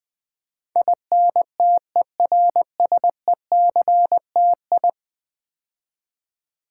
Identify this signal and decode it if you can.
Morse code, heard as a keyed tone, about 20 wpm, 700 Hz. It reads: INTERSECTI
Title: Build A Word - 5 Letters At A Time - 12-Letter Words - 20wpm